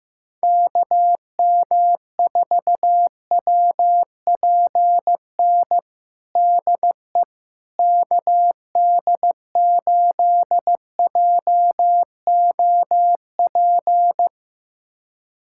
Morse code, heard as a keyed tone, about 15 wpm, 700 Hz. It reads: KM4WPN DE KD8JOP
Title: USA Callsign DE Callsign - 6 Characters - 15wpm